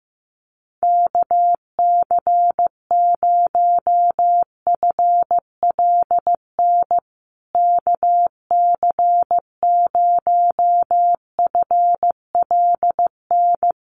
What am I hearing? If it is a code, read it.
Morse code, heard as a keyed tone, about 15 wpm, 700 Hz. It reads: KC0FLN KC0FLN
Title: USA Call Signs Repeated 3x 15wpm